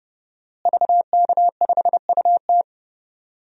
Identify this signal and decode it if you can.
Morse code, heard as a keyed tone, about 30 wpm, 700 Hz. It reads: VK5UT
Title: DXCC-100 Call Signs - 5 Characters - Repeated 2x - 30wpm